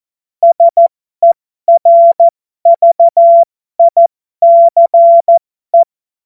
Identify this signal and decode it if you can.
Morse code, heard as a keyed tone, about 14 wpm, spaced as for 12 wpm, 680 Hz. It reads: SERVICE
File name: Morsekurs / Witze